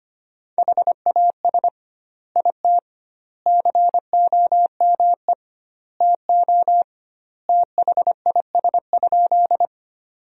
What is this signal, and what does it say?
Morse code, heard as a keyed tone, about 25 wpm, 700 Hz. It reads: HAS IT COME TO THIS?
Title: 2x Word-Spacing - Sentences from Top 100 Words - 25wpm